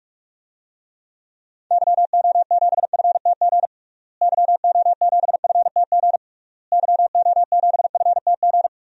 Morse code, heard as a keyed tone, about 46 wpm, 700 Hz. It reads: YO7FTG YO7FTG YO7FTG
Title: DXCC-100 Call Signs - 6 Characters - Repeated 3x - 45wpm